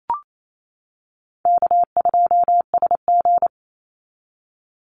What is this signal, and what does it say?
Morse code, heard as a keyed tone, about 28 wpm, 700 Hz. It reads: K2SG
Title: Easy Call Signs - 28wpm